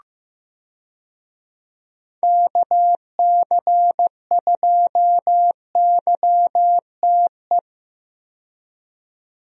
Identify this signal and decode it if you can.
Morse code, heard as a keyed tone, about 15 wpm, 700 Hz. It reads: KC2YTE
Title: USA Call Signs - 6 Characters - 15wpm